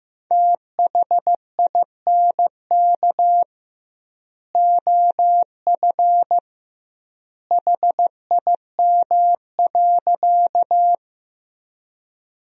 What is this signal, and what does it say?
Morse code, heard as a keyed tone, about 15 wpm, 700 Hz. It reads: THINK OF HIM.
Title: 2x Word-Spacing - Sentences from Top 200 Words - 15wpm